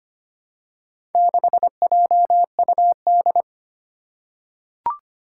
Morse code, heard as a keyed tone, about 25 wpm, 700 Hz. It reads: 6JUD
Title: Four Challenging Letter-Number Combinations 25wpm